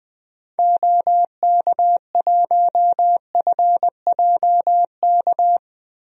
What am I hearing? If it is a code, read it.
Morse code, heard as a keyed tone, about 20 wpm, 700 Hz. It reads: OK1FJK
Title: DXCC-100 Call Signs - 6 Characters - Repeated 3x - 20wpm